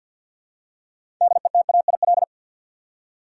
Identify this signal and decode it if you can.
Morse code, heard as a keyed tone, about 53 wpm, 700 Hz. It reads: DETAIL
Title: Six Letter Words - 50wpm